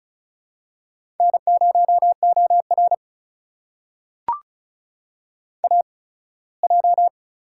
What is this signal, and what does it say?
Morse code, heard as a keyed tone, about 36 wpm, spaced as for 21 wpm, 700 Hz. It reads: N0OR AJ
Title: USA Call Signs - 4 Characters - 8x Character Spacing - 35wpm